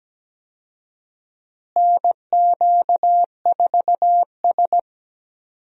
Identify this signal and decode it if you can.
Morse code, heard as a keyed tone, about 17 wpm, 700 Hz. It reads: NQ4S
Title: USA Call Signs - 4 Characters - Repeated 2x - 17wpm